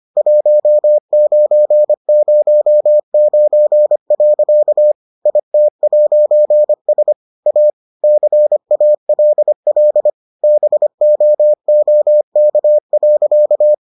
Morse code, heard as a keyed tone, about 25 wpm, 600 Hz. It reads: 1909. IT'S A CALL BOOK.